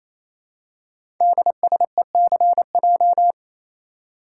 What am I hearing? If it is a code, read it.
Morse code, heard as a keyed tone, about 27 wpm, 700 Hz. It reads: DSECJ
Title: Code Groups - 5 Letters - 28wpm